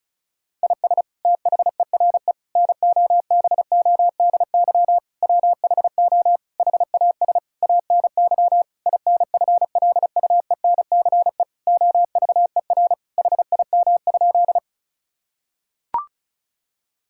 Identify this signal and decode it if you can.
Morse code, heard as a keyed tone, about 34 wpm, 700 Hz. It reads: IS THERE NOBODY WHO HAS ANY INFLUENCE OVER HIM?
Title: Sentences from Top 1600 Words 35wpm